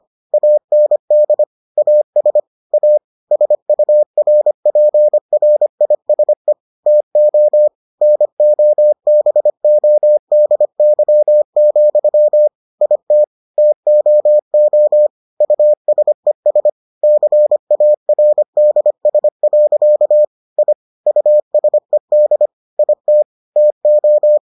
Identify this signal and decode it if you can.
Morse code, heard as a keyed tone, about 25 wpm, 600 Hz. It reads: AND AS A SURPRISE TO NOBODY, IT TOO USES CARDS. I USED IT TO